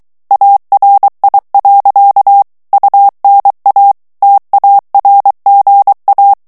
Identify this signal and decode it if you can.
Morse code, heard as a keyed tone, about 23 wpm, 800 Hz. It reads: ARI. UNA TARGA
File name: ITALIEN texte clair 2 vitesse 22 mots minute